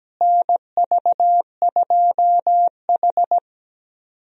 Morse code, heard as a keyed tone, about 17 wpm, 700 Hz. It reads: NV2H
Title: USA Call Signs - 4 Characters - Repeated 2x - 17wpm